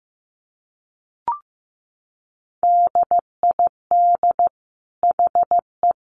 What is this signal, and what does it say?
Morse code, heard as a keyed tone, about 15 wpm, 700 Hz. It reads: DID HE
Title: Sentences from Top 800 Words 15wpm